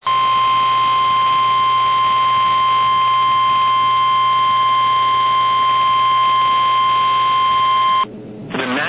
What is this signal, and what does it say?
Emergency Alert System (SAME) tones (the 1050 Hz attention tone), with noise. It is heard from 0.0 to 8.1 s.
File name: eas / KNCYKBIE / KNCYKBIE